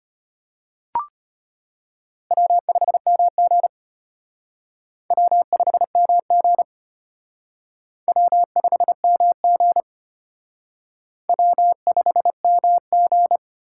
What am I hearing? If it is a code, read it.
Morse code, heard as a keyed tone, about 32 wpm, 700 Hz. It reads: W5MG W5MG W5MG W5MG